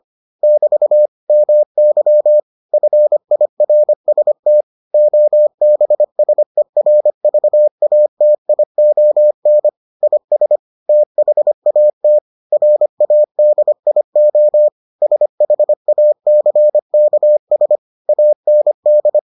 Morse code, heard as a keyed tone, about 25 wpm, 600 Hz. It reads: <BT> MY FIRST OBSERVATION IS THAT RADIO SHACKS AND